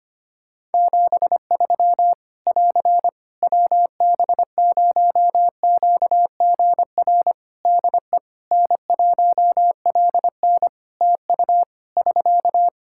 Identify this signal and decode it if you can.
Morse code, heard as a keyed tone, about 25 wpm, 700 Hz. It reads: 73 <AR> WB0QGR DE N1LN TU <SK>
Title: Practice QSO - Exchange 4 - Ending - 25wpm